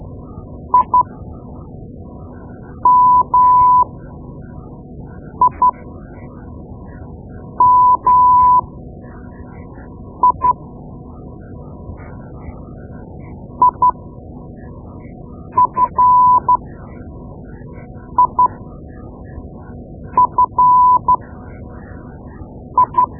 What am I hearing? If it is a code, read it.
Morse code, heard as a keyed tone, about 12 wpm, spaced as for 4 wpm, 1000 Hz. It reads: IMIMI IFIFI
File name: Morse Lessons / Morse Lessons / Tape-2B